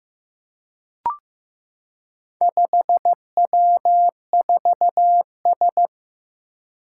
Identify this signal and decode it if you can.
Morse code, heard as a keyed tone, about 15 wpm, 700 Hz. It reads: 5W4S
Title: Four Challenging Letter-Number Combinations 15wpm